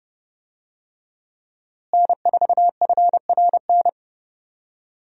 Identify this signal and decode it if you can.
Morse code, heard as a keyed tone, about 30 wpm, 700 Hz. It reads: N4FRN
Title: USA Call Signs - 5 Characters - Repeated 3x - 30wpm